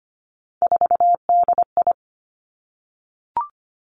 Morse code, heard as a keyed tone, about 25 wpm, 700 Hz. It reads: 4DI
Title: Three Challenging Letter-Number Combinations 25wpm